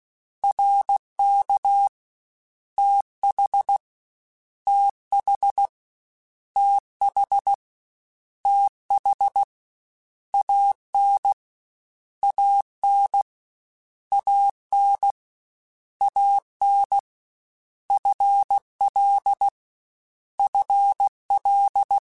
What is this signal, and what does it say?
Morse code, heard as a keyed tone, about 16 wpm, 800 Hz. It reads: RK TH TH TH TH AN AN AN AN FL FL
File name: Contrasts / Contrasts-mp3s / 2LtrContrasts @15